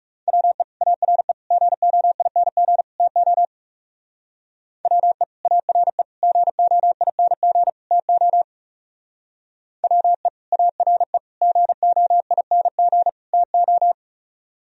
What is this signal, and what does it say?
Morse code, heard as a keyed tone, about 35 wpm, 700 Hz. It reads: WE ARE GOING TO WE ARE GOING TO WE ARE GOING TO
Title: Speed-Racing - Sets of 4 Words 30wpm